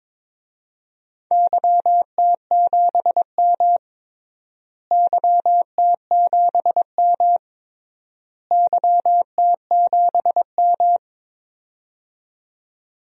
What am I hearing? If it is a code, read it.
Morse code, heard as a keyed tone, about 22 wpm, 700 Hz. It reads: YT7M YT7M YT7M
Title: DXCC-100 Call Signs - 4 Characters - Repeated 3x - 22wpm